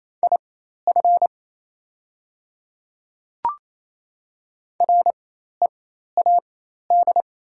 Morse code, heard as a keyed tone, about 28 wpm, spaced as for 13 wpm, 700 Hz. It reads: IF READ
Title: 4x Character Spacing - Top 300 Words - 28wpm